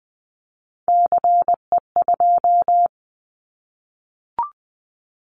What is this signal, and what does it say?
Morse code, heard as a keyed tone, about 20 wpm, 700 Hz. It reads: CE2
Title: Callsign Prefixes for Top 100 DXCC - 20wpm